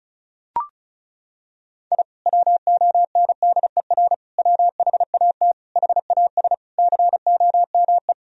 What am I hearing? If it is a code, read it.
Morse code, heard as a keyed tone, about 36 wpm, 700 Hz. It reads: I WONDER WHAT HAS COME
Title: Sentences from Top 1000 Words - Review Part 4 - 35wpm